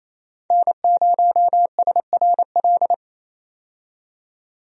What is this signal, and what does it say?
Morse code, heard as a keyed tone, about 28 wpm, 700 Hz. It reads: N0SRL
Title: USA Call Signs - 5 Characters - 28wpm